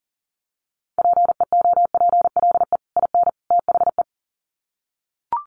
Morse code, heard as a keyed tone, about 40 wpm, 700 Hz. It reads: PEOPLE IN THE
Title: Sets of 3 Words 40wpm